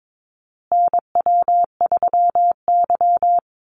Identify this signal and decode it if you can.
Morse code, heard as a keyed tone, about 22 wpm, 690 Hz. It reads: NW3Y
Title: Call Signs Repeated 2x - 22wpm